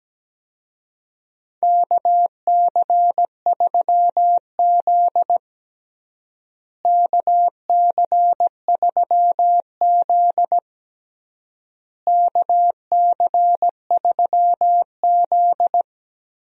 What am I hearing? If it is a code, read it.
Morse code, heard as a keyed tone, about 17 wpm, 700 Hz. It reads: KC3Z KC3Z KC3Z
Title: USA Call Signs - 4 Characters - Repeated 3x - 17wpm